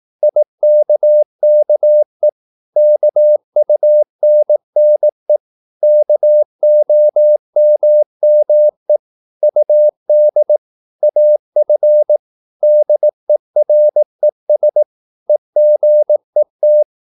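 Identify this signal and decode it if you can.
Morse code, heard as a keyed tone, about 18 wpm, 600 Hz. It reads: IKKE KUNNE KOMME UD AF DERES EGET
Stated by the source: Slow (QRS) Morse practice at 18 wpm, CW med Gnister.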